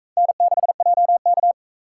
Morse code, heard as a keyed tone, about 38 wpm, 700 Hz. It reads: N6JK